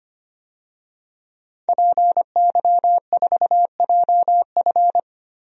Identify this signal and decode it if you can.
Morse code, heard as a keyed tone, about 25 wpm, 700 Hz. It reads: PY4JF
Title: DXCC-100 Call Signs - 5 Characters - 25wpm